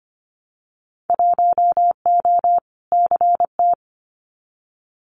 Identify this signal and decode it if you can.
Morse code, heard as a keyed tone, about 25 wpm, 700 Hz. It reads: 1O CT
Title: Winter Field Day Exchanges 25wpm